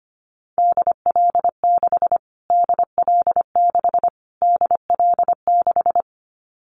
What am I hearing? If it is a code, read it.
Morse code, heard as a keyed tone, about 25 wpm, 700 Hz. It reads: DL6 DL6 DL6
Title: Callsign Prefixes for Top 25 DXCC - Repeated 3x - 25wpm